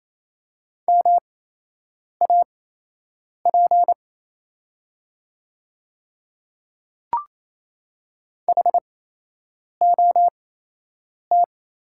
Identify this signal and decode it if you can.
Morse code, heard as a keyed tone, about 28 wpm, spaced as for 8 wpm, 700 Hz. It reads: MAP HOT
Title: 8x Character Spacing - Top 500 Words - 28wpm